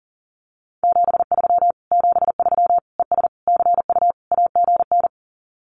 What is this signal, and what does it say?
Morse code, heard as a keyed tone, about 40 wpm, 700 Hz. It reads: 73 73 ES CU AGN
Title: Common QSO Phrases Encore 40wpm